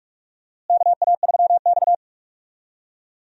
Morse code, heard as a keyed tone, about 46 wpm, 700 Hz. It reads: KA3X
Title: USA Call Signs - 4 Characters - Repeated 3x - 45wpm